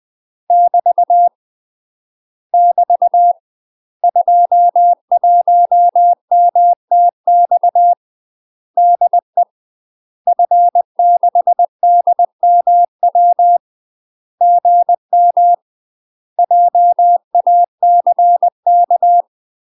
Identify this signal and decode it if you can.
Morse code, heard as a keyed tone, about 20 wpm, 700 Hz. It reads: <BT> <BT> 21MTX DE F6DMW GM JACK
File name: UK MASTER QSOs TRAINING / 20WPM / UK MASTER QSOs 0053